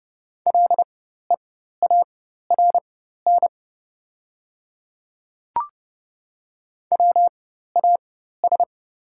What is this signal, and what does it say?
Morse code, heard as a keyed tone, about 30 wpm, spaced as for 14 wpm, 700 Hz. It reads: LEARN WAS
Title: 4x Character Spacing - Top 300 Words - 30wpm